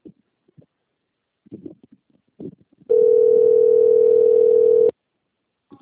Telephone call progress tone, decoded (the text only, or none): ringback tone